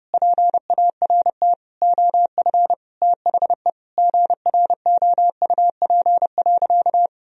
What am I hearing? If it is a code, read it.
Morse code, heard as a keyed tone, about 32 wpm, 700 Hz. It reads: PART OF THE GROUP.
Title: Sentences from Top 1000 Words - Review Part 1 - 30wpm